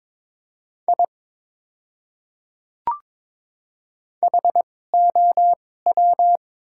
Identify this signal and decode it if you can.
Morse code, heard as a keyed tone, about 21 wpm, spaced as for 16 wpm, 700 Hz. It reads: I HOW
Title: 2x Character Spacing - Top 100 Words - 22wpm